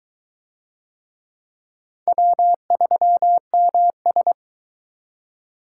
Morse code, heard as a keyed tone, about 22 wpm, 700 Hz. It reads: W3MS